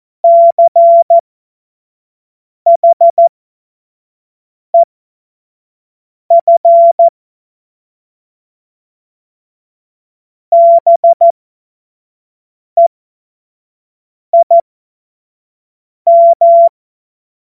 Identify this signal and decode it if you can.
Morse code, heard as a keyed tone, about 14 wpm, spaced as for 5 wpm, 680 Hz. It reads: CHEF BEIM